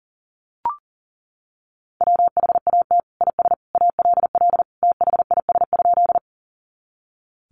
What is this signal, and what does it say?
Morse code, heard as a keyed tone, about 40 wpm, 700 Hz. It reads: WHAT IS ALL THIS?
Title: Sentences from Top 100 Words 40wpm